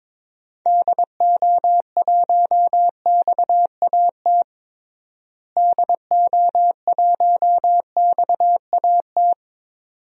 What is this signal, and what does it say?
Morse code, heard as a keyed tone, about 22 wpm, 700 Hz. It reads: DO1XAT DO1XAT
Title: DXCC-100 Call Signs - 6 Characters - Repeated 2x - 22wpm